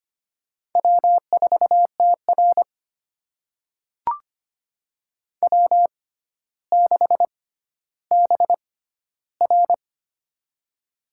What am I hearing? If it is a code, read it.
Morse code, heard as a keyed tone, about 25 wpm, spaced as for 13 wpm, 700 Hz. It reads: W4TR W6BR
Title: USA Call Signs - 4 Characters - 6x Character Spacing - 25wpm